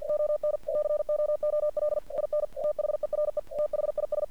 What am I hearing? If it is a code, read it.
Morse code, heard as a keyed tone, about 49 wpm, 600 Hz. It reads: ON COOL IN THERE THIS
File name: archived / CRSnet